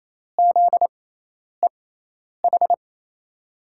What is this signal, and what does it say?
Morse code, heard as a keyed tone, about 28 wpm, spaced as for 10 wpm, 700 Hz. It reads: ZEH
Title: USA Call Signs - 6 Characters - 6x Character Spacing - 28wpm